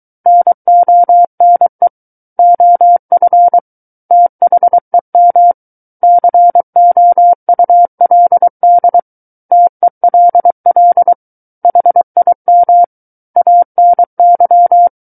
Morse code, heard as a keyed tone, about 23 wpm, 700 Hz. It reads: NONE OF THEM COULD TELL HIM ANY
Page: SKCC Learning Center - Morse Code Practice Files